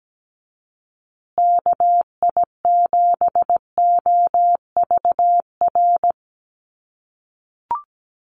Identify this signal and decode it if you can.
Morse code, heard as a keyed tone, about 17 wpm, 700 Hz. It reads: KI7OVR
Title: USA Call Signs Repeated 3x 17wpm